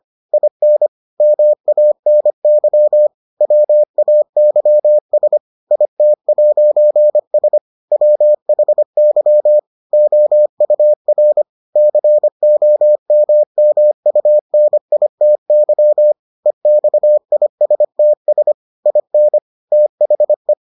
Morse code, heard as a keyed tone, about 25 wpm, 600 Hz. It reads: IN MANY WAYS IT'S WHY OUR COMMUNITY EXISTS IN THE